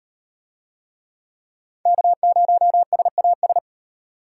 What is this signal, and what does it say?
Morse code, heard as a keyed tone, about 36 wpm, 700 Hz. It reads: K0SAS